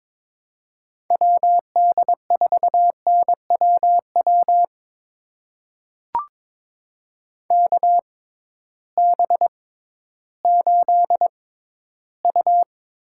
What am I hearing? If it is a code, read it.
Morse code, heard as a keyed tone, about 21 wpm, spaced as for 13 wpm, 700 Hz. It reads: WD4NWW KB8U
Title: USA Call Signs - 6 Characters - 6x Character Spacing - 22wpm